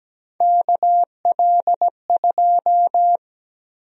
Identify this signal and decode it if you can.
Morse code, heard as a keyed tone, about 17 wpm, 700 Hz. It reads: KL2
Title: USA Call Sign Prefixes - 17wpm